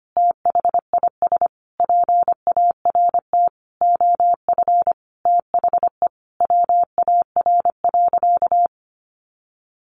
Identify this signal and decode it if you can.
Morse code, heard as a keyed tone, about 25 wpm, 700 Hz. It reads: THIS PART OF THE WAR.
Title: Sentences from Top 1500 Words 25wpm